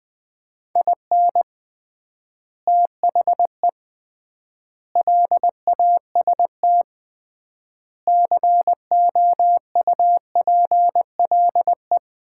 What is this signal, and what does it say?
Morse code, heard as a keyed tone, about 20 wpm, 700 Hz. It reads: IN THE LAST COUPLE
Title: 3x Word-Spacing - Sets of 5 Words - 20wpm